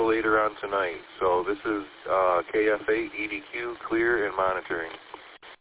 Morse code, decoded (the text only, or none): none